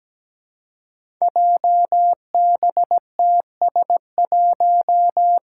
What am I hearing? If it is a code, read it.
Morse code, heard as a keyed tone, about 17 wpm, 700 Hz. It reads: JBTS1
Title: Five Challenging Letter-Number Combinations 17wpm